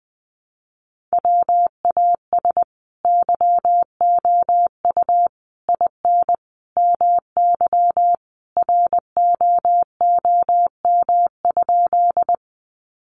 Morse code, heard as a keyed tone, about 20 wpm, 700 Hz. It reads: WAS YOU IN MY ROOM?
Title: Sentences from Top 300 Words 20wpm